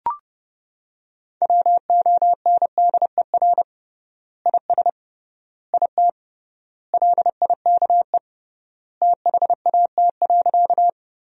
Morse code, heard as a keyed tone, about 30 wpm, 700 Hz. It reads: WONDER IS IT LIKE THAT.
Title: 3x Word-Spacing - Sentences from Top 500 Words - 30wpm